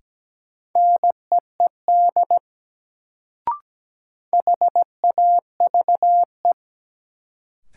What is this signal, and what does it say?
Morse code, heard as a keyed tone, about 17 wpm, 700 Hz. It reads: NEED HAVE
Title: Sets of 2 Words - New Word Drill - 17wpm